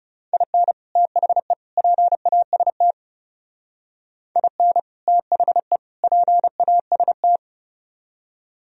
Morse code, heard as a keyed tone, about 36 wpm, 700 Hz. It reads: IN THE PAST IN THE PAST
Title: Speed-Racing - Sets of 3 Words 30wpm